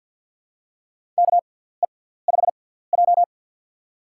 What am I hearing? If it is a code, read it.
Morse code, heard as a keyed tone, about 47 wpm, spaced as for 17 wpm, 700 Hz. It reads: KE5J
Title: USA Call Signs - 4 Characters - 6x Character Spacing - 50wpm